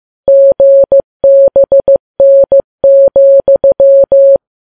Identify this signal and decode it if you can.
Morse code, heard as a keyed tone, about 15 wpm, 550 Hz. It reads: GBN,
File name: Files Web tls / Morse on9fv / morse15kps